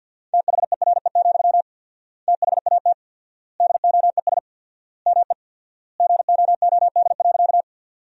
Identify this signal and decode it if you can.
Morse code, heard as a keyed tone, about 51 wpm, 700 Hz. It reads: THERE, THAT DOES ME GOOD.
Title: 4x Word-Spacing - Sentences from Top 200 Words - 50wpm